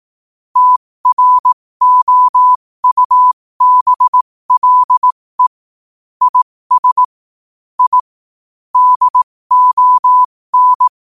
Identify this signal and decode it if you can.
Morse code, heard as a keyed tone, about 18 wpm, spaced as for 15 wpm, 1000 Hz. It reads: TROUBLE IS I DON
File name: qotdfiles / 15wpm